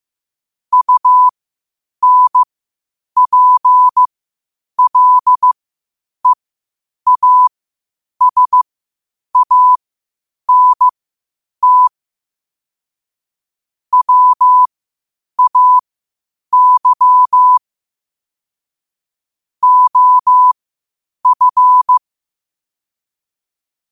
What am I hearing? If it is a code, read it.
Morse code, heard as a keyed tone, about 15 wpm, spaced as for 8 wpm, 1000 Hz. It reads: UNPLEASANT WAY OF
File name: qotdfiles / 8wpm